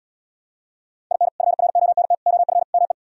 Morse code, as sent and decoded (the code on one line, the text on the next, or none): .. -. / ..-. .-. --- -. - / --- ..-. / -- .
IN FRONT OF ME